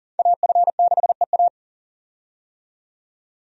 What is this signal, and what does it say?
Morse code, heard as a keyed tone, about 40 wpm, 700 Hz. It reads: AF6EA
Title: USA Call Signs - 5 Characters - 40wpm